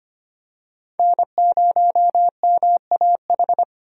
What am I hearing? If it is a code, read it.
Morse code, heard as a keyed tone, about 25 wpm, 700 Hz. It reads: N0MAH
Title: USA Call Signs - 5 Characters - Repeated 3x - 25wpm